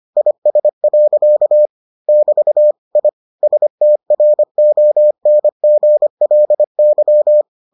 Morse code, heard as a keyed tone, about 25 wpm, 600 Hz. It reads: IS. <BT> I STRONGLY